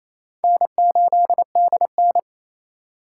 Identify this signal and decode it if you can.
Morse code, heard as a keyed tone, about 37 wpm, 700 Hz. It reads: N8DN